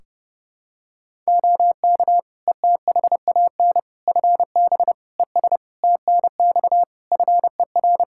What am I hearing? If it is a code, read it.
Morse code, heard as a keyed tone, about 32 wpm, 700 Hz. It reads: OK ETHAN FB ES TNX FER
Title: Practice QSO - Exchange 2 - Info - 30wpm